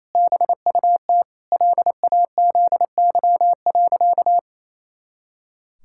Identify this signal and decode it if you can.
Morse code, heard as a keyed tone, about 28 wpm, 700 Hz. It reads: BUT LAZY.
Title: Pangram Challenge v2 - 28wpm